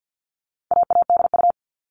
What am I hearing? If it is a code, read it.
Morse code, heard as a keyed tone, about 50 wpm, 700 Hz. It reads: AADU
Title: All Cut Numbers - 4-Digit 50wpm